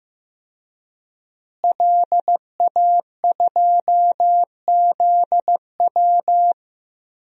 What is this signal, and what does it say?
Morse code, heard as a keyed tone, about 15 wpm, 700 Hz. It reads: LA2ZW
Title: DXCC-100 Call Signs - 5 Characters - Repeated 2x - 15wpm